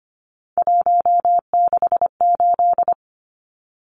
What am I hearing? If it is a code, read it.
Morse code, heard as a keyed tone, about 25 wpm, 700 Hz. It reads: 168
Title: Numbers - 3-Digit 25wpm